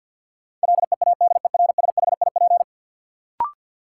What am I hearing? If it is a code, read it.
Morse code, heard as a keyed tone, about 53 wpm, 700 Hz. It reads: LEADERSHIP
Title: Sentences from Top 1600 Words - New Word Drill - 50wpm